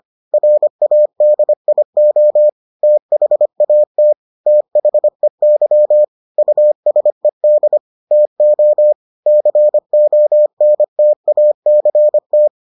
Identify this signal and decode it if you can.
Morse code, heard as a keyed tone, about 25 wpm, 600 Hz. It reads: RADIO THAT THEY USED TO CONTACT